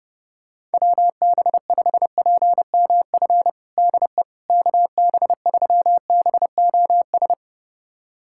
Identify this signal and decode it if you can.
Morse code, heard as a keyed tone, about 30 wpm, 700 Hz. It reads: WB5PMF DE KB3BOS
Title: USA Callsign DE Callsign 30wpm